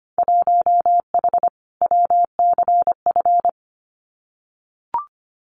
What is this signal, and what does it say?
Morse code, heard as a keyed tone, about 25 wpm, 690 Hz. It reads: 1H WCF
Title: Winter Field Day Exchanges 25wpm